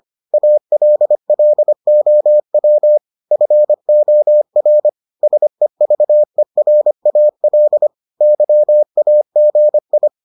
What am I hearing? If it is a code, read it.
Morse code, heard as a keyed tone, about 25 wpm, 600 Hz. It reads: ALLOW FOR SEVERAL YAGI